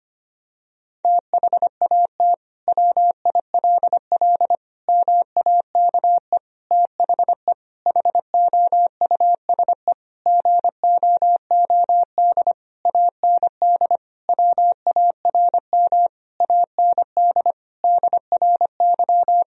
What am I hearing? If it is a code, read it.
Morse code, heard as a keyed tone, about 25 wpm, 700 Hz. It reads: THAT WILL MAKE THE HOUSE GOOD AND WARM AND DRY
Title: Sentences from Top 1000 Words - Review Part 3 - 25wpm